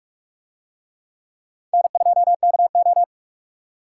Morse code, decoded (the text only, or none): N2KO